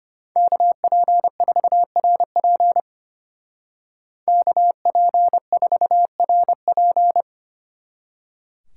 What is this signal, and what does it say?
Morse code, heard as a keyed tone, about 32 wpm, 700 Hz. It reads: KP4RP KP4RP